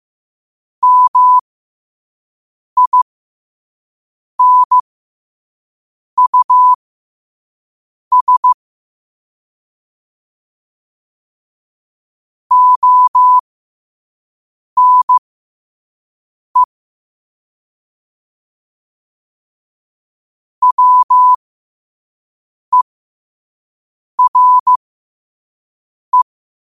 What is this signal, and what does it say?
Morse code, heard as a keyed tone, about 15 wpm, spaced as for 5 wpm, 1000 Hz. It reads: MINUS ONE WERE
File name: qotdfiles / 5wpm